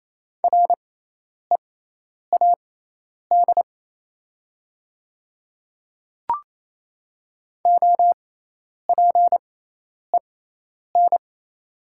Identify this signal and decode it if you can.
Morse code, heard as a keyed tone, about 28 wpm, spaced as for 10 wpm, 700 Hz. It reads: READ OPEN
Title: 6x Character Spacing - Top 300 Words - 28wpm